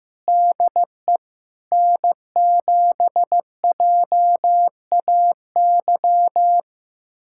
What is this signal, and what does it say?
Morse code, heard as a keyed tone, about 15 wpm, 700 Hz. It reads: DE N7JAY
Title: USA Callsign DE Callsign - 5 Characters - 15wpm